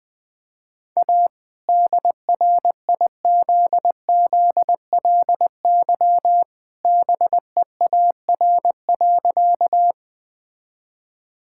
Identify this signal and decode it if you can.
Morse code, heard as a keyed tone, about 20 wpm, 700 Hz. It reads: A DRIZZLY BEAR.
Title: All-Age Appropriate Jokes - 20wpm